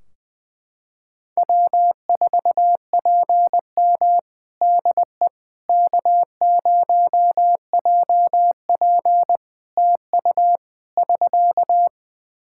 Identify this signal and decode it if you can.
Morse code, heard as a keyed tone, about 20 wpm, 700 Hz. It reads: W4PM DE K0JP TU <SK>
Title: Practice QSO - Exchange 4 - Ending - 20wpm